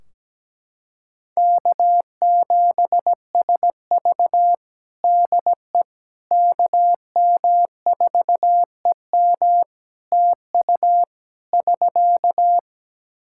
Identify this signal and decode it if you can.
Morse code, heard as a keyed tone, about 17 wpm, 700 Hz. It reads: K7SV DE KM4EM TU <SK>
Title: Practice QSO - Exchange 4 - Ending - 17wpm